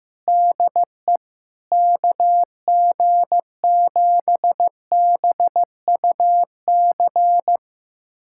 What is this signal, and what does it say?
Morse code, heard as a keyed tone, about 15 wpm, 700 Hz. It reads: DE KG7BUC
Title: USA Callsign DE Callsign 15wpm